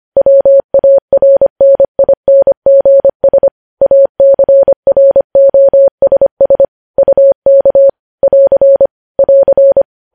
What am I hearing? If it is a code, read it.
Morse code, heard as a keyed tone, about 25 wpm, 550 Hz. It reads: WARNINGS ACROSS UK <AR> <AR>
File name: podcast / News headlines-1x-25x25